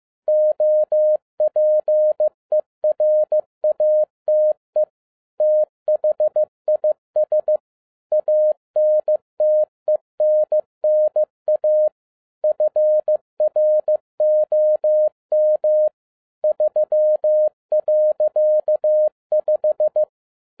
Morse code, heard as a keyed tone, about 15 wpm, 610 Hz. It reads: OPERATE THIS ANTENNA FROM 3.5
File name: 2012-01-mp3 / n07-ARA JR0DIL-15wpm